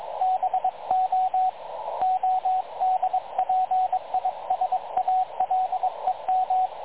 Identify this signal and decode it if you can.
Morse code, heard as a keyed tone, about 22 wpm, 750 Hz, with radio noise. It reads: BO ODPISALEM
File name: dlazaawansowanych / nagraniazpasma / websdr recording 3567.0kHz